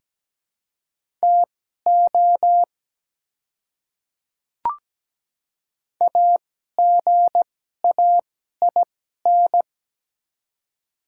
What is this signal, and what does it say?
Morse code, heard as a keyed tone, about 17 wpm, spaced as for 12 wpm, 700 Hz. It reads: TO AGAIN
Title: 2x Character Spacing - Top 300 Words - 17wpm